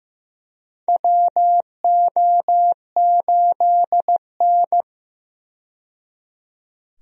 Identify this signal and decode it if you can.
Morse code, heard as a keyed tone, about 15 wpm, 700 Hz. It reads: WO8N
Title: USA Call Signs - 4 Characters - 15wpm